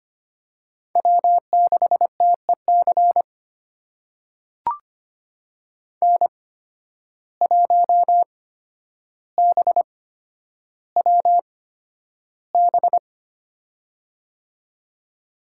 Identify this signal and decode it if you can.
Morse code, heard as a keyed tone, about 25 wpm, spaced as for 11 wpm, 700 Hz. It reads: W6TEC N1BWB
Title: USA Call Signs - 5 Characters - 8x Character Spacing - 25wpm